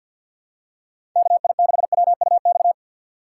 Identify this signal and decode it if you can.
Morse code, heard as a keyed tone, about 53 wpm, 700 Hz. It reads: KI6WUX
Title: USA Call Signs - 6 Characters - 3x Character Spacing - 50wpm